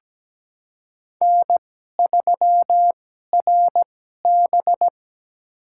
Morse code, heard as a keyed tone, about 17 wpm, spaced as for 12 wpm, 700 Hz. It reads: N3RB
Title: USA Call Signs - 2x Character Spacing - 17wpm